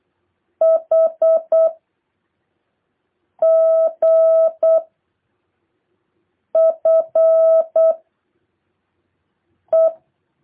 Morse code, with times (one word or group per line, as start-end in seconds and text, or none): 0.6-1.7 H
3.4-4.8 G
6.6-7.9 F
9.7-9.9 E